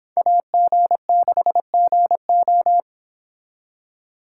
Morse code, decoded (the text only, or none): AG6GO